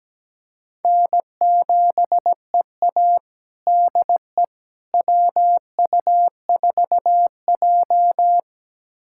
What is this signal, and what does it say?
Morse code, heard as a keyed tone, about 17 wpm, 700 Hz. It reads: N7EA DE WU4J
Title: USA Callsign DE Callsign - 4 Characters - 17wpm